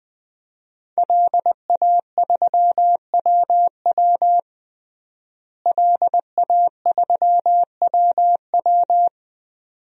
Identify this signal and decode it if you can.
Morse code, heard as a keyed tone, about 20 wpm, 700 Hz. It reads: LA3WW LA3WW
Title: DXCC-100 Call Signs - 5 Characters - Repeated 2x - 20wpm